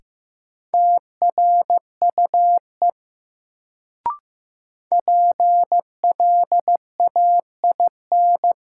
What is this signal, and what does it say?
Morse code, heard as a keyed tone, about 16 wpm, 700 Hz. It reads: TRUE PLAIN
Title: Sentences from Top 400 Words - New Word Drill - 15wpm